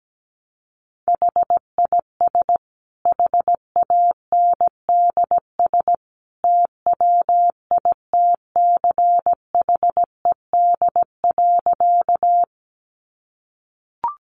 Morse code, heard as a keyed tone, about 17 wpm, 700 Hz. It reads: HIS HANDS TWITCHED.
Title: A Sound Of Thunder 17wpm